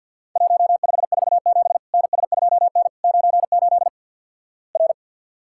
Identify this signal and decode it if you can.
Morse code, heard as a keyed tone, about 52 wpm, 700 Hz. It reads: 1547 NS2N 98 R
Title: Worked All Europe DX Contest - QTC Exchange with Repeats - 50wpm